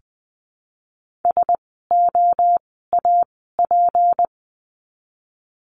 Morse code, heard as a keyed tone, about 20 wpm, spaced as for 14 wpm, 700 Hz. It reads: SOAP
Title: 153 Lesson 25f - Letter p - Words 20wpm